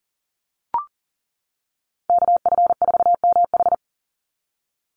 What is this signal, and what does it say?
Morse code, heard as a keyed tone, about 40 wpm, 700 Hz. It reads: KF4MH
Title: NA SOTA Chaser Callsigns 40wpm